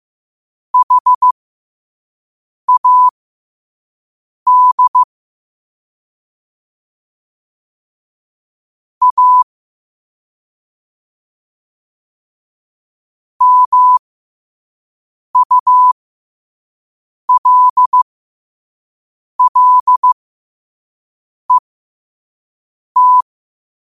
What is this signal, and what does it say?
Morse code, heard as a keyed tone, about 15 wpm, spaced as for 5 wpm, 1000 Hz. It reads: HAD A MULLET
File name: qotdfiles / 5wpm